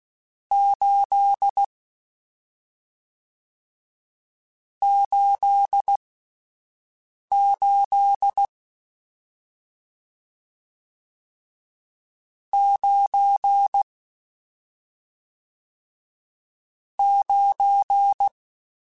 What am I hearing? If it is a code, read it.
Morse code, heard as a keyed tone, about 16 wpm, spaced as for 5 wpm, 800 Hz. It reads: 8 88 9 9
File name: Misc / Misc-mp3s / Numb @5